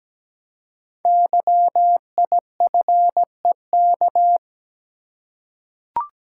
Morse code, heard as a keyed tone, about 17 wpm, 700 Hz. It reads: YIFEK
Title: Code Groups - 5 Letters - 17wpm